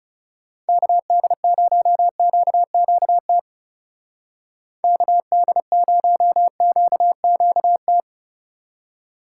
Morse code, heard as a keyed tone, about 37 wpm, 700 Hz. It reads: KD0QQT KD0QQT
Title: Speed Racing - USA 2x3 Call Signs 30wpm